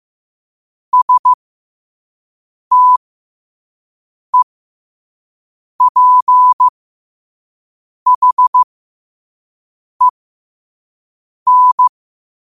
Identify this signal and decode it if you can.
Morse code, heard as a keyed tone, about 15 wpm, spaced as for 5 wpm, 1000 Hz. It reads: STEPHEN
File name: qotdfiles / 5wpm